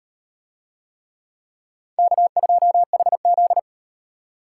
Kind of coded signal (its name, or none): Morse code